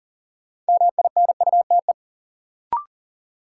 Morse code, heard as a keyed tone, about 40 wpm, 700 Hz. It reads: MINUTE
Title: Sentences from Top 500 Words - New Word Drill - 40wpm